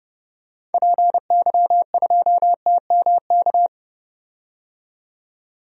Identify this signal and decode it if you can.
Morse code, heard as a keyed tone, about 30 wpm, 700 Hz. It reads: PY2TMK
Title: DXCC-100 Call Signs - 6 Characters - 30wpm